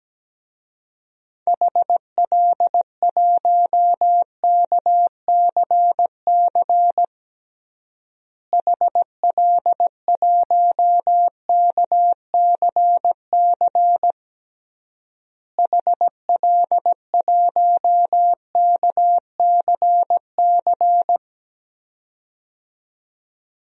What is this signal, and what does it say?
Morse code, heard as a keyed tone, about 17 wpm, 700 Hz. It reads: HL1KCC HL1KCC HL1KCC
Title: DXCC-100 Call Signs - 6 Characters - Repeated 3x - 17wpm